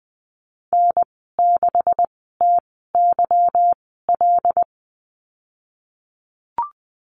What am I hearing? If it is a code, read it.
Morse code, heard as a keyed tone, about 20 wpm, spaced as for 14 wpm, 700 Hz. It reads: N6TYL
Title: 147 Lesson 24g - Letter y - Call Signs 20wpm